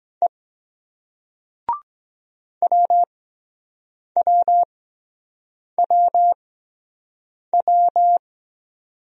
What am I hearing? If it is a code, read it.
Morse code, heard as a keyed tone, about 23 wpm, spaced as for 7 wpm, 700 Hz. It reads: E WWWW